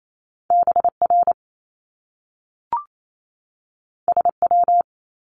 Morse code, heard as a keyed tone, about 28 wpm, 700 Hz. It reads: BR SW
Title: Top Two Letter Combinations in Words 28wpm